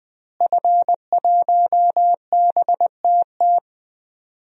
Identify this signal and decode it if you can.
Morse code, heard as a keyed tone, about 20 wpm, 700 Hz. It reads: F1BTT
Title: DXCC-100 Call Signs - 5 Characters - Repeated 3x - 20wpm